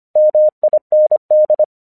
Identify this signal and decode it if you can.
Morse code, heard as a keyed tone, about 25 wpm, 610 Hz. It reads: MIND
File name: 1 English 25wpm / AQC26